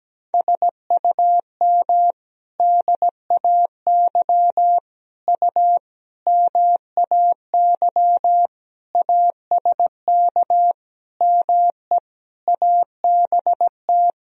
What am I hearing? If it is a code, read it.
Morse code, heard as a keyed tone, about 17 wpm, 700 Hz. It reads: SUM DAY U MAY ASK ME ABT
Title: Sentences from Top 200 Words and Abbreviations 17wpm